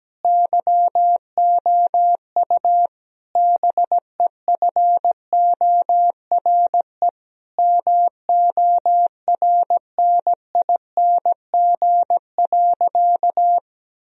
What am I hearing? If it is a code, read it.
Morse code, heard as a keyed tone, about 17 wpm, 700 Hz. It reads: YOU BEFORE MORNING.
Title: Sentences from Top 1000 Words - Review Part 2 - 17wpm